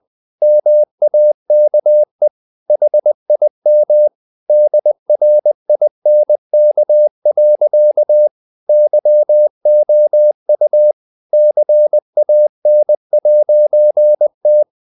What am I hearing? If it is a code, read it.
Morse code, heard as a keyed tone, about 20 wpm, 600 Hz. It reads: MAKE HIM DRINK. YOU CAN'T
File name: CW / Practice / EnglishSayings20